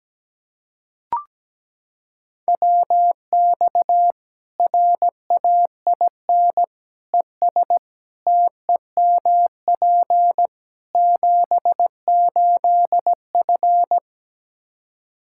Morse code, heard as a keyed tone, about 17 wpm, 700 Hz. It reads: WX RAIN ES TEMP 78F
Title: K1USN SST QSO Practice 17wpm